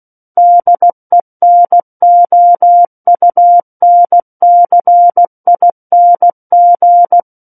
Morse code, heard as a keyed tone, about 16 wpm, 700 Hz. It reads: DENOUNCING
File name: Book One / 07-Chapter 7 - 16 WPM